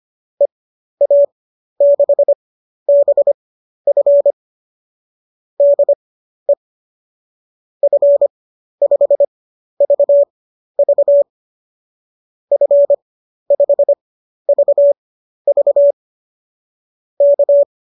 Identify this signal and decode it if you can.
Morse code, heard as a keyed tone, about 25 wpm, spaced as for 12 wpm, 590 Hz. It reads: EA6BF DE F5VV F5VV K
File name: qso009 12